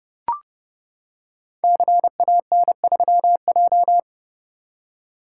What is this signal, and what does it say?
Morse code, heard as a keyed tone, about 30 wpm, 700 Hz. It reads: CAN3J
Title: Code Groups - 5 Letter:Numbers - 30wpm